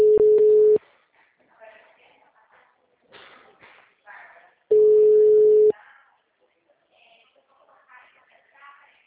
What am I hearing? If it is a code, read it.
Ringback tone (Europe, 425 Hz), with speech.
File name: out- -2013- - - .79375